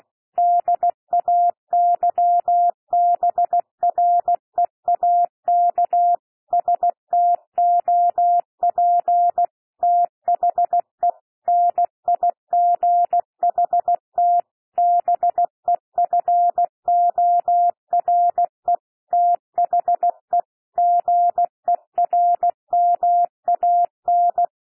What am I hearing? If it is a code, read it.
Morse code, heard as a keyed tone, about 16 wpm, 700 Hz. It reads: DAYBREAK STOP THE NIGHT BEFORE THE GERMAN
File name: cwmp3 / P1 SOUND PASSAGE 16 wpm